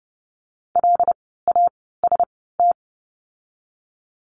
Morse code, heard as a keyed tone, about 30 wpm, spaced as for 17 wpm, 700 Hz. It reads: LAST
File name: Lesson 17f - Letter u - Words 30wpm